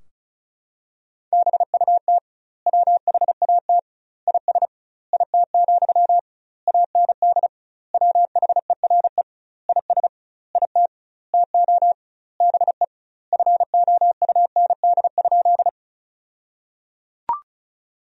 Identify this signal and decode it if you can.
Morse code, heard as a keyed tone, about 36 wpm, 700 Hz. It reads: BUT WHAT IS IT, AND WHERE IS IT TO BE FOUND?
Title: 2x Word-Spacing - Sentences from Top 300 Words - 35wpm